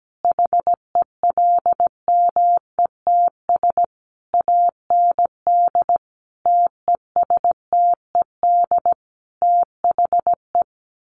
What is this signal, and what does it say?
Morse code, heard as a keyed tone, about 17 wpm, 700 Hz. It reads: HELMETS AND TESTED THE
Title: A Sound Of Thunder 17wpm